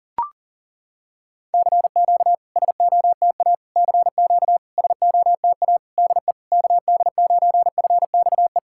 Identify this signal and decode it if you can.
Morse code, heard as a keyed tone, about 39 wpm, 690 Hz. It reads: CQ SOTA CQ SOTA DE KD9FXE
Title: SOTA QSO Practice 40wpm